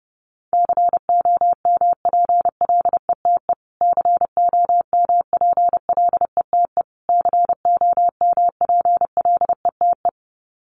Morse code, heard as a keyed tone, about 30 wpm, 700 Hz. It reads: COMPLETE COMPLETE COMPLETE
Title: Top 500 Words Repeated 30wpm